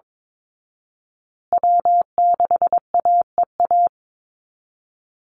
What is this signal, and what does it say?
Morse code, heard as a keyed tone, about 22 wpm, 700 Hz. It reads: W6AEA
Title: Call Signs Repeated 2x - 22wpm